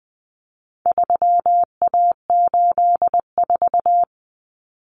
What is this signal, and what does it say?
Morse code, heard as a keyed tone, about 20 wpm, 700 Hz. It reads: 3A84
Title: Common Cut Numbers - 4-Digit 20wpm